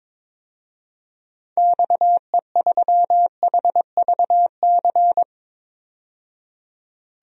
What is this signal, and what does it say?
Morse code, heard as a keyed tone, about 22 wpm, 700 Hz. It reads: XE3HVC
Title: DXCC-100 Call Signs - 6 Characters - 22wpm